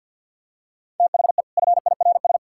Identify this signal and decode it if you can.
Morse code, heard as a keyed tone, about 51 wpm, 700 Hz. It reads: THE FIRS
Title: Sentences from Top 1000 Words - Review Part 6 - 50wpm